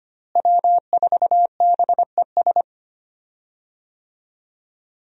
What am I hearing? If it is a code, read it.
Morse code, heard as a keyed tone, about 25 wpm, 700 Hz. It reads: W4BES
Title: USA Call Signs - 5 Characters - Repeated 2x - 25wpm